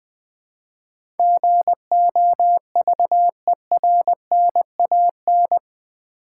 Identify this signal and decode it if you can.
Morse code, heard as a keyed tone, about 20 wpm, 700 Hz. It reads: GOVERNAN
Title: Build A Word - 4 Letters At A Time - 10-Letter Words - 20wpm